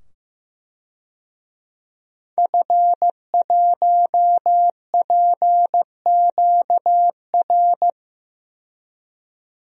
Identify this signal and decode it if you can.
Morse code, heard as a keyed tone, about 15 wpm, 700 Hz. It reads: F1PQR
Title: DXCC-100 Call Signs - 5 Characters - 15wpm